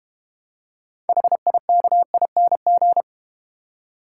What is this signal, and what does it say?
Morse code, heard as a keyed tone, about 32 wpm, 700 Hz. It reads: HIKING